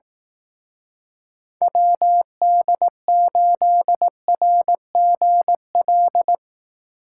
Morse code, heard as a keyed tone, about 19 wpm, 700 Hz. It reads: WD8RGL
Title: Speed Racing - USA 2x3 Call Signs 15wpm